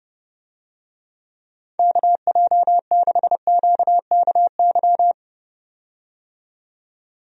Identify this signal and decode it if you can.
Morse code, heard as a keyed tone, about 30 wpm, 700 Hz. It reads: KJ6QKY
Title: USA Call Signs - 6 Characters - 30wpm